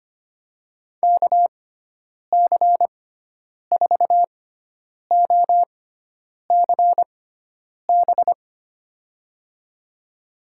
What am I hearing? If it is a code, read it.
Morse code, heard as a keyed tone, about 25 wpm, spaced as for 9 wpm, 700 Hz. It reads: KC4OCB
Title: USA Call Signs - 6 Characters - 6x Character Spacing - 25wpm